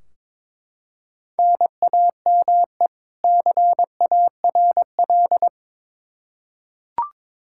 Morse code, heard as a keyed tone, about 22 wpm, 700 Hz. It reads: NAME CARL
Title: Practice QSO - Exchange 1 - Beginning - 22wpm